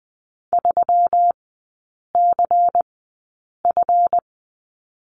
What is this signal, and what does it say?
Morse code, heard as a keyed tone, about 20 wpm, spaced as for 8 wpm, 700 Hz. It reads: 3CF
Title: 172 Lesson 28e - Number 9 - Three Characters 20wpm